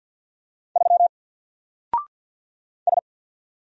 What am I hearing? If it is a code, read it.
Morse code, heard as a keyed tone, about 48 wpm, 700 Hz. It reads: 3 S
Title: Single Character (Letter:Number:Punctuation:Prosign) Alt 50wpm